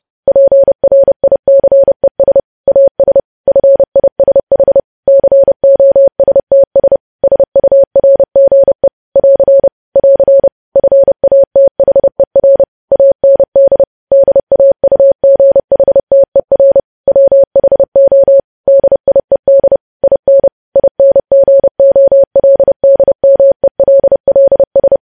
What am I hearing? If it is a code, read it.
Morse code, heard as a keyed tone, about 30 wpm, 550 Hz. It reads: PRICES AS FISH COSTS SURGE <AR> <AR> FATHER AND DAUGHTER WHO DIED IN INGOLDMELLS
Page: news headlines in morse code at 30 wpm